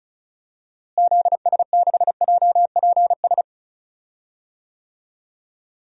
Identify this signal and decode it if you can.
Morse code, heard as a keyed tone, about 36 wpm, 700 Hz. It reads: ZS6JPS
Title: DXCC-100 Call Signs - 6 Characters - Repeated 3x - 35wpm